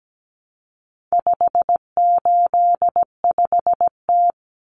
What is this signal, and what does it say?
Morse code, heard as a keyed tone, about 17 wpm, 700 Hz. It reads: 585T
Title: Common Cut Numbers - 4-Digit 17wpm